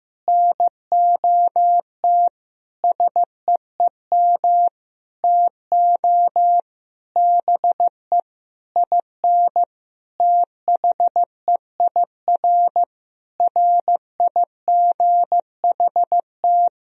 Morse code, heard as a keyed tone, about 16 wpm, 700 Hz. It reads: NOT SEEM TO BE IN THEIR RIGHT
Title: Sentences from Top 1000 Words - Review Part 1 - 15wpm